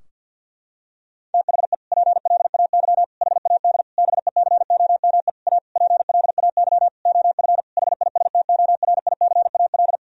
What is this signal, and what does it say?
Morse code, heard as a keyed tone, about 50 wpm, 700 Hz. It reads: THE PLAY HAD BECOME A PLAY OF HISTORICAL
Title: Sentences from Top 1000 Words - Review Part 4 - 50wpm